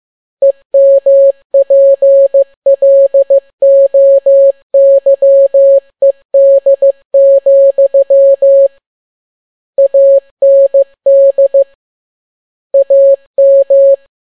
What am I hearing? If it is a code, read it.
Morse code, heard as a keyed tone, about 15 wpm, 550 Hz. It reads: EMPLOYED, AND AM